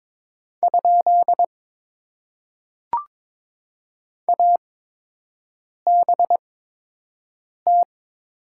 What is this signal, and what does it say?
Morse code, heard as a keyed tone, about 22 wpm, spaced as for 6 wpm, 700 Hz. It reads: ? ABT
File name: Top 90 QSO Elements - 8x Character Spacing - 22wpm